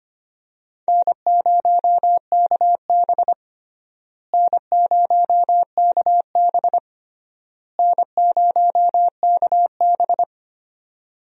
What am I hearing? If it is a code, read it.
Morse code, heard as a keyed tone, about 25 wpm, 700 Hz. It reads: N0KB N0KB N0KB
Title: USA Call Signs - 4 Characters - Repeated 3x - 25wpm